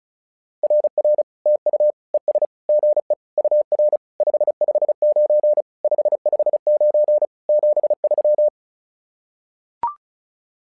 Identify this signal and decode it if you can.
Morse code, heard as a keyed tone, about 36 wpm, 600 Hz. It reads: RR TU ES GE UR 559 559 73
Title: SOTA QSO Practice 35wpm